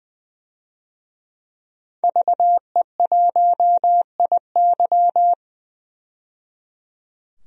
Morse code, heard as a keyed tone, about 20 wpm, 700 Hz. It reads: VE1IY
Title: DXCC-100 Call Signs - 5 Characters - 20wpm